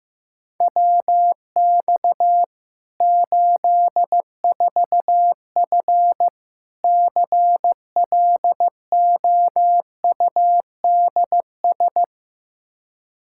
Morse code, heard as a keyed tone, about 15 wpm, 700 Hz. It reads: WX 84F CLOUDS
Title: Practice QSO - Exchange 2 - Info - 15wpm